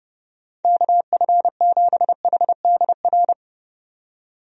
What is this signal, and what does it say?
Morse code, heard as a keyed tone, about 29 wpm, 700 Hz. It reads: KF7HDR
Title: USA Call Signs - 6 Characters - 30wpm